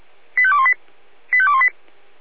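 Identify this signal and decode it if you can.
Five-tone selective call (ZVEI), heard as quick runs of five tones, tones of about 70 ms. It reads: 85218 85218
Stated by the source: ZVEI five-tone selective call 85280 received on an SDR.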